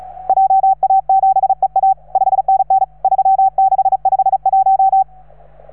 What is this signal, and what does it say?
Morse code, heard as a keyed tone, about 35 wpm, 780 Hz. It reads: JA7EA 5NN 3651